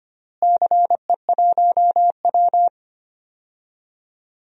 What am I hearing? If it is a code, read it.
Morse code, heard as a keyed tone, about 25 wpm, 700 Hz. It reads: CE1W
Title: DXCC-100 Call Signs - 4 Characters - Repeated 2x - 25wpm